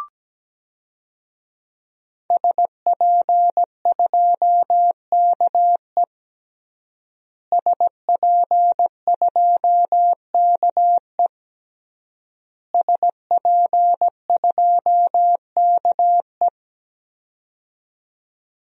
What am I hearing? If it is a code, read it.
Morse code, heard as a keyed tone, about 17 wpm, 700 Hz. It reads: SP2KE SP2KE SP2KE
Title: DXCC-100 Call Signs - 5 Characters - Repeated 3x - 17wpm